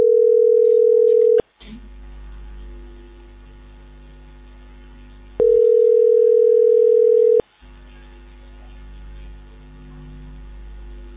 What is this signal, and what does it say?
Ringback tone (North America), 440 + 480 Hz.